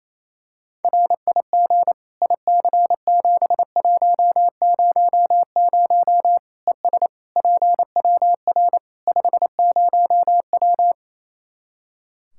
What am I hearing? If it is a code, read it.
Morse code, heard as a keyed tone, about 28 wpm, 700 Hz. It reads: RIG IC7100 ES PWR 50W
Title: Practice QSO - Exchange 2 - Info - 28wpm